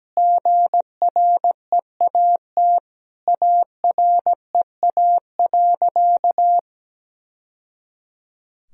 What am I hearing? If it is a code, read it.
Morse code, heard as a keyed tone, about 17 wpm, 700 Hz. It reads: GREAT AREA.
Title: Sentences from Top 1000 Words - Review Part 2 - 17wpm